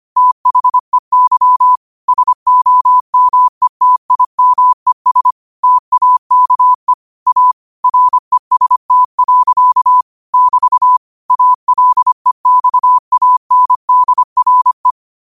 Morse code, heard as a keyed tone, about 25 wpm, 1000 Hz. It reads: THEY SOMETIMES TAKE A REST. <BT> ALEXANDRE
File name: qotdfiles / 25wpm